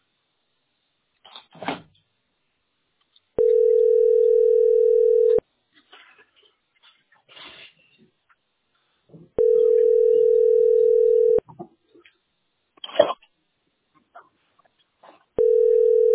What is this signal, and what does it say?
Ringback tone (North America), 440 + 480 Hz.